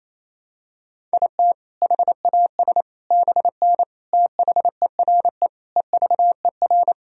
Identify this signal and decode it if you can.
Morse code, heard as a keyed tone, about 28 wpm, 700 Hz. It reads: IT HAS BN THERE EVER
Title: Sentences from Top 400 Words and Abbreviations 28wpm